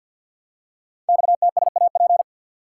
Morse code, heard as a keyed tone, about 50 wpm, 700 Hz. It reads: XTSAP
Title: Code Groups - 5 Letters - 50wpm